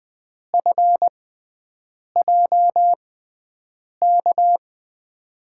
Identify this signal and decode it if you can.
Morse code, heard as a keyed tone, about 20 wpm, spaced as for 7 wpm, 700 Hz. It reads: FJK
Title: USA Call Signs - 6 Characters - 6x Character Spacing - 20wpm